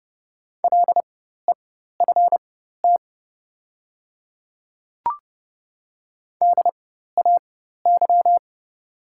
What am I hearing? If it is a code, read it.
Morse code, heard as a keyed tone, about 30 wpm, spaced as for 14 wpm, 700 Hz. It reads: LEFT DAY
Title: 4x Character Spacing - Top 300 Words - 30wpm